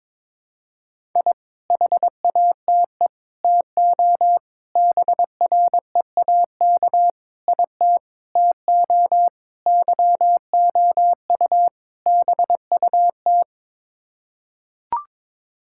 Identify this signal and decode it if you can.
Morse code, heard as a keyed tone, about 22 wpm, 700 Hz. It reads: I HATE TO BREAK IT TO YOU BUT
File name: practice / 22wpm / Sets of 8 Words - 22wpm